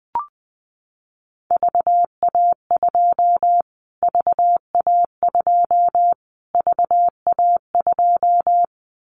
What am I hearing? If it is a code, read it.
Morse code, heard as a keyed tone, about 20 wpm, 700 Hz. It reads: VA2 VA2 VA2
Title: Callsign Prefixes for Top 25 DXCC - Repeated 3x - 20wpm